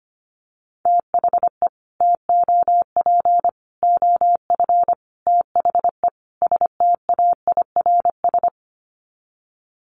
Morse code, heard as a keyed tone, about 25 wpm, 700 Hz. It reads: THE TOP OF THE STAIRS
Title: Sets of 5 Words Encore 25wpm